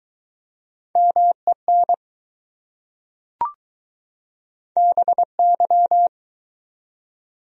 Morse code, heard as a keyed tone, about 20 wpm, 700 Hz. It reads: MEN BY